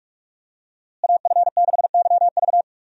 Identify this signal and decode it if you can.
Morse code, heard as a keyed tone, about 46 wpm, 700 Hz. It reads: AF6YV
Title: USA Call Signs - 5 Characters - Repeated 2x - 45wpm